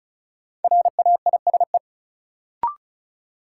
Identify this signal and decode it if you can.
Morse code, heard as a keyed tone, about 36 wpm, 700 Hz. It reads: RAISE
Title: Sentences from Top 1400 Words - New Word Drill Alt - 35wpm